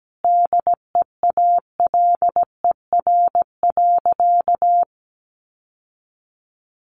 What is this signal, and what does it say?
Morse code, heard as a keyed tone, about 17 wpm, 700 Hz. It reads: DEALER.
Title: Sentences from Top 1100 Words 17wpm